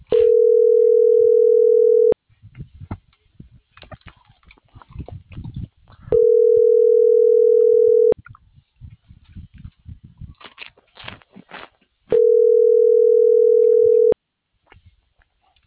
Ringback tone (North America), 440 + 480 Hz.